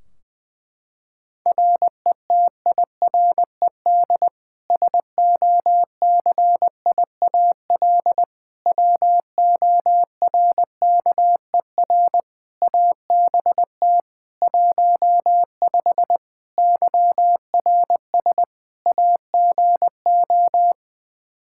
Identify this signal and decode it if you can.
Morse code, heard as a keyed tone, about 20 wpm, 700 Hz. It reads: RETIRED SOCIAL WORKER ABT 15 YRS AGO
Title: K1USN SST QSO Practice 20wpm